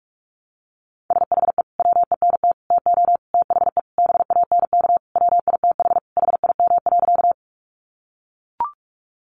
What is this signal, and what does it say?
Morse code, heard as a keyed tone, about 44 wpm, 700 Hz. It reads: SHE WENT TO THE BANK WITH HIM.
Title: Sentences from Top 900 Words 45wpm